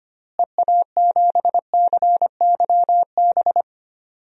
Morse code, heard as a keyed tone, about 25 wpm, 700 Hz. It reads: EA7CYB
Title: DXCC-100 Call Signs - 6 Characters - Repeated 3x - 25wpm